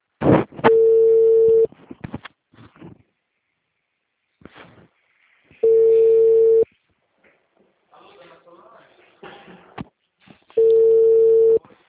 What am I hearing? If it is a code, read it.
Ringback tone (China, 450 Hz).